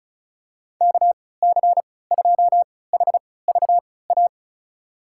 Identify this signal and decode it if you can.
Morse code, heard as a keyed tone, about 36 wpm, spaced as for 20 wpm, 700 Hz. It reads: KC2HVA
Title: USA Call Signs - 6 Characters - 3x Character Spacing - 35wpm